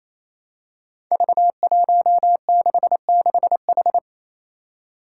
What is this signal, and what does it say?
Morse code, heard as a keyed tone, about 27 wpm, 700 Hz. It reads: V166H
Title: Five Challenging Letter-Number Combinations 28wpm